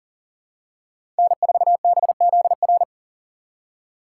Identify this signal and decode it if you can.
Morse code, heard as a keyed tone, about 40 wpm, 700 Hz. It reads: N4BZR
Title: USA Call Signs - 5 Characters - 4x Character Spacing - 40wpm